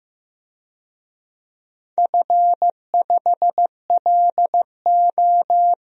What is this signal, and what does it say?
Morse code, heard as a keyed tone, about 15 wpm, 700 Hz. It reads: F5LO
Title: DXCC-100 Call Signs - 4 Characters - 15wpm